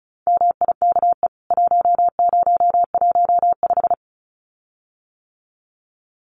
Morse code, heard as a keyed tone, about 34 wpm, 700 Hz. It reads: MIKE 1015
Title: CWT Exchange 35wpm